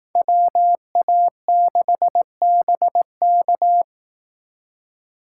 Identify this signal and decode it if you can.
Morse code, heard as a keyed tone, about 18 wpm, 700 Hz. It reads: WA6BK
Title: Speed Racing - USA 2x2 Call Signs 15wpm